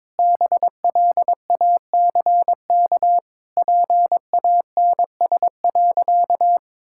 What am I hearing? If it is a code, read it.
Morse code, heard as a keyed tone, about 22 wpm, 700 Hz. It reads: BLACK PANS.
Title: Pangram Challenge v2 - 22wpm